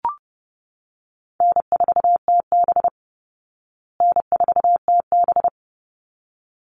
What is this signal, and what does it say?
Morse code, heard as a keyed tone, about 30 wpm, 700 Hz. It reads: N4TB N4TB
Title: Easy Call Signs Repeated 2x - 30wpm